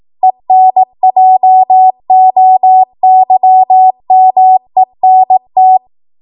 Morse code, heard as a keyed tone, about 18 wpm, 750 Hz. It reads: ENJOYMENT
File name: Morse / 18WPM